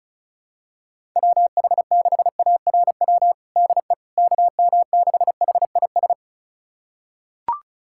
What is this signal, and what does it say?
Morse code, heard as a keyed tone, about 36 wpm, 700 Hz. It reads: WH6ARW DE KM6HIS
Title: USA Callsign DE Callsign - 6 Characters - 35wpm